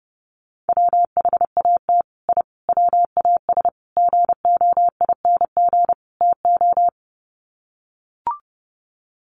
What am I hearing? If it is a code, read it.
Morse code, heard as a keyed tone, about 30 wpm, 700 Hz. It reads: WHAT I WAS GOING TO
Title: Sets of 5 Words Encore2 30wpm